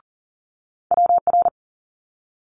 Morse code, heard as a keyed tone, about 39 wpm, 700 Hz. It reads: WR
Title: Multi-letter Phonemes Quick Fire 40wpm